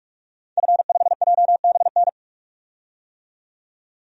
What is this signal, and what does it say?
Morse code, heard as a keyed tone, about 42 wpm, 700 Hz. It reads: F5JBN
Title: DXCC-100 Call Signs - 5 Characters - 45wpm